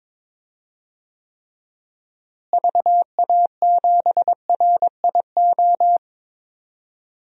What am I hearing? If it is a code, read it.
Morse code, heard as a keyed tone, about 22 wpm, 700 Hz. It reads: VA7RIO
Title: DXCC-100 Call Signs - 6 Characters - Repeated 2x - 22wpm